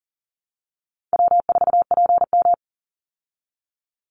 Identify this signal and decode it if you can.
Morse code, heard as a keyed tone, about 40 wpm, 700 Hz. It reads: W4PM
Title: Call Signs Repeated 3x - 40wpm